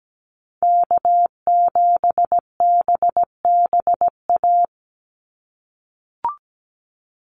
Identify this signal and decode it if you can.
Morse code, heard as a keyed tone, about 17 wpm, 700 Hz. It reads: K7BBA
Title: USA Call Signs 17wpm